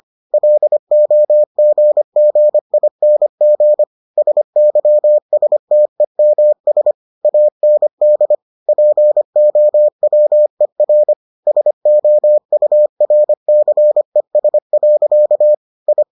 Morse code, heard as a keyed tone, about 25 wpm, 600 Hz. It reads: LOGGING SYSTEMS AND POWER SOURCES. I